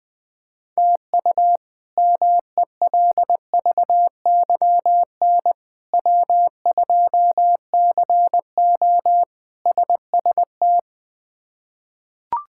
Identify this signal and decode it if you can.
Morse code, heard as a keyed tone, about 20 wpm, 700 Hz. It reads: TU MELVYN W2CO SST
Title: K1USN SST QSO Practice - 20wpm